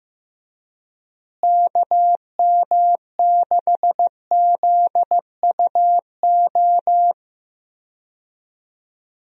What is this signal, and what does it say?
Morse code, heard as a keyed tone, about 15 wpm, 700 Hz. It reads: KM6ZUO
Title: USA Call Signs - 6 Characters - Repeated 2x - 15wpm